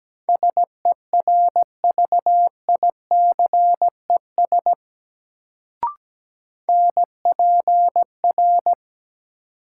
Morse code, heard as a keyed tone, about 17 wpm, 700 Hz. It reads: SERVICES NPR
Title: Sets of 5 Words - New Word Drill - 17wpm